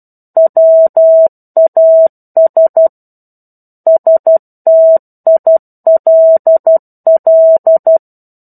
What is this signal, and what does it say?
Morse code, heard as a keyed tone, about 12 wpm, 650 Hz. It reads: WAS STILL
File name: 01-Chapter-01 12 WPM